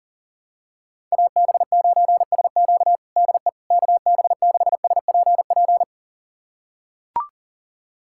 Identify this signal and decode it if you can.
Morse code, heard as a keyed tone, about 40 wpm, 700 Hz. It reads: AB9SQ DE KB6SPP
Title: USA Callsign DE Callsign 40wpm